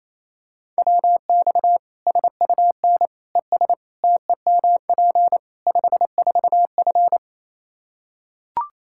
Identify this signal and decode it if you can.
Morse code, heard as a keyed tone, about 28 wpm, 700 Hz. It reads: WX SUN ES TEMP 54F
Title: Practice QSO - Exchange 2 - Info - 28wpm